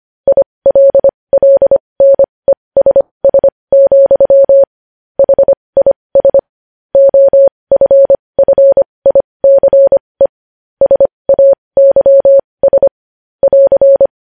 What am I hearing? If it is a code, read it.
Morse code, heard as a keyed tone, about 25 wpm, spaced as for 20 wpm, 550 Hz. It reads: ILLNESS, HIS OFFICE SAYS <AR>
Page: news headlines in morse code at 20 wpm